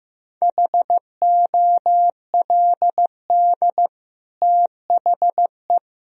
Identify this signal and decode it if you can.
Morse code, heard as a keyed tone, about 16 wpm, 700 Hz. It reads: HOLD THE
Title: Sentences from Top 1000 Words - Review Part 2 - 15wpm